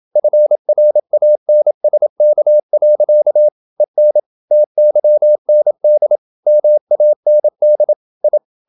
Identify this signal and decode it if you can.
Morse code, heard as a keyed tone, about 27 wpm, 600 Hz. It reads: FRANSK. EN TYND MAND I
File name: Ferie QRQ 27wpm